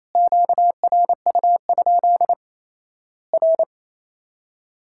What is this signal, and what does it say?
Morse code, heard as a keyed tone, about 27 wpm, 700 Hz. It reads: QRU? R
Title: Worked All Europe DX Contest - QTC Exchange - 28wpm